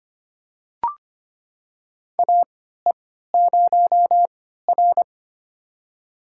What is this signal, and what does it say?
Morse code, heard as a keyed tone, about 25 wpm, spaced as for 14 wpm, 700 Hz. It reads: AE0R
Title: USA Call Signs - 4 Characters - 3x Character Spacing - 25wpm